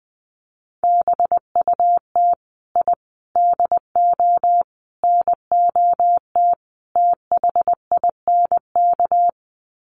Morse code, heard as a keyed tone, about 20 wpm, 700 Hz. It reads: BUT I DO NOT THINK
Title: Sets of 5 Words 20wpm